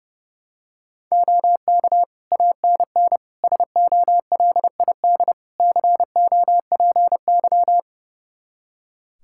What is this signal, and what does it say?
Morse code, heard as a keyed tone, about 32 wpm, 700 Hz. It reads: OK ANN SOLID COPY
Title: K1USN SST QSO Practice 30wpm